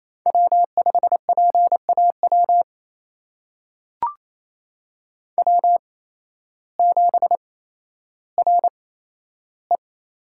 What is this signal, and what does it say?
Morse code, heard as a keyed tone, about 28 wpm, spaced as for 13 wpm, 700 Hz. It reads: W5PAW W7RE
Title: USA Call Signs - 8x Character Spacing - 28wpm